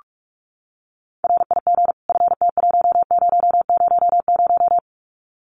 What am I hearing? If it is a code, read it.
Morse code, heard as a keyed tone, about 44 wpm, 700 Hz. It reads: RIG FT1000
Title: Common QSO Phrases 45wpm